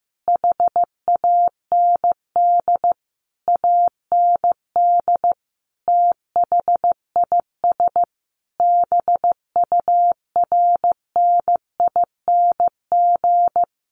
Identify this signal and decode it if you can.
Morse code, heard as a keyed tone, about 15 wpm, 700 Hz. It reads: HAND AND THIS BURNING
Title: A Sound Of Thunder 15wpm